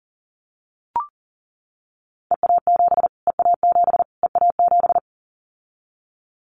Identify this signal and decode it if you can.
Morse code, heard as a keyed tone, about 37 wpm, 700 Hz. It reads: EA7 EA7 EA7
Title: Callsign Prefixes for Top 25 DXCC - Repeated 3x - 40wpm